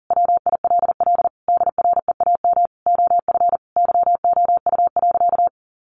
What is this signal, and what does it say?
Morse code, heard as a keyed tone, about 40 wpm, 700 Hz. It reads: WILL DREAM OF YOU.
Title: Sentences from Top 1500 Words 40wpm